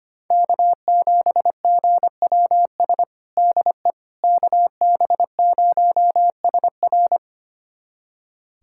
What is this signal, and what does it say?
Morse code, heard as a keyed tone, about 25 wpm, 700 Hz. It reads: K7GWS DE KB0SR
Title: USA Callsign DE Callsign - 5 Characters - 25wpm